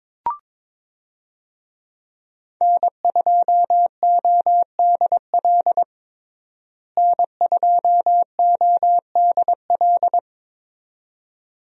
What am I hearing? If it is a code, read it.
Morse code, heard as a keyed tone, about 22 wpm, 700 Hz. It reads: N2ODL N2ODL
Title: USA Call Signs - 5 Characters - Repeated 2x - 22wpm